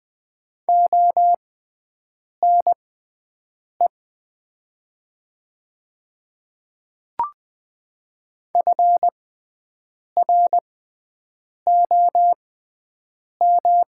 Morse code, heard as a keyed tone, about 20 wpm, spaced as for 7 wpm, 700 Hz. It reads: ONE FROM
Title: 6x Character Spacing - Top 100 Words - 20wpm